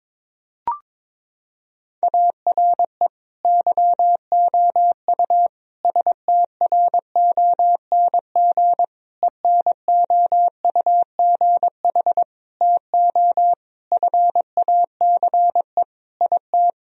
Morse code, heard as a keyed tone, about 22 wpm, 700 Hz. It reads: ARE YOU STRONG ENOUGH TO FACE IT
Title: Sentences from Top 1000 Words - Review Part 3 - 22wpm